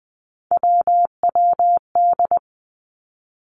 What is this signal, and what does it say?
Morse code, heard as a keyed tone, about 20 wpm, 700 Hz. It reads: WWD
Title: Three Challenging Letter-Number Combinations 20wpm